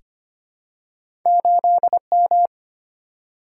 Morse code, heard as a keyed tone, about 25 wpm, 700 Hz. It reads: 8M
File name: Two Letter-Number Combinations - 25wpm